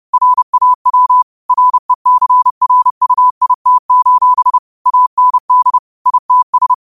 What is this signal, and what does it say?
Morse code, heard as a keyed tone, about 30 wpm, 1000 Hz. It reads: RAW RECRUIT: AND ITS
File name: qotdfiles / 30wpm